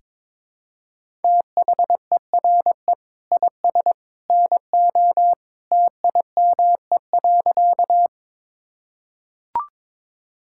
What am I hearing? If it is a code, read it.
Morse code, heard as a keyed tone, about 22 wpm, 700 Hz. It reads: THERE IS NO TIME.
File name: practice / 22wpm / Sentences with 2-Word Phrases - 22wpm